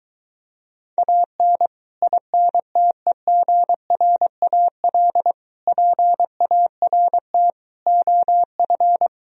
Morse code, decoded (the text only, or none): AN INTEGRAL PART OF